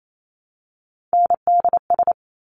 Morse code, heard as a keyed tone, about 28 wpm, 700 Hz. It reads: NDS
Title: Top Three Letter Combinations in Words 28wpm